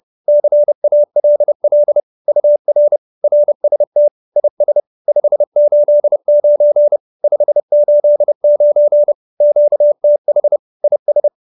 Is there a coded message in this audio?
Yes: Morse code from 0.3 to 11.3 s, heard as a keyed tone, about 30 wpm, 590 Hz. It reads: CALL UR RST IS 589 589 QTH IS